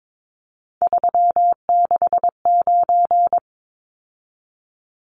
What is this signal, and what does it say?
Morse code, heard as a keyed tone, about 22 wpm, 700 Hz. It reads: 369
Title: Numbers - 3-Digit 22wpm